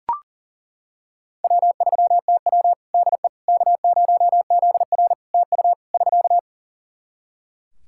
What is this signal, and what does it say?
Morse code, heard as a keyed tone, about 40 wpm, 700 Hz. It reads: W3TW DE K0ZR TU <SK>
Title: K1USN SST QSO Practice 40wpm